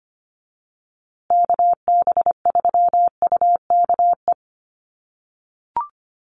Morse code, heard as a keyed tone, about 25 wpm, 700 Hz. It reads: KB3UKE
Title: USA Call Signs Repeated 2x 25wpm